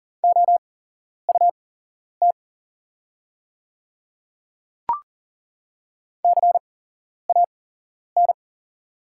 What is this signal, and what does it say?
Morse code, heard as a keyed tone, about 40 wpm, spaced as for 11 wpm, 700 Hz. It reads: OUT CAN
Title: 8x Character Spacing - Top 200 Words - 40wpm